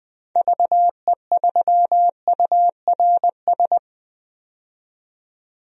Morse code, heard as a keyed tone, about 20 wpm, 700 Hz. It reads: VE3URS
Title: DXCC-100 Call Signs - 6 Characters - 20wpm